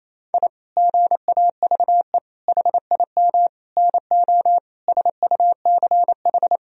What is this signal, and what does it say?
Morse code, heard as a keyed tone, about 28 wpm, 700 Hz. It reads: I GAVE HIM NO SUCH
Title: Sentences from Top 1000 Words - Review Part 2 - 28wpm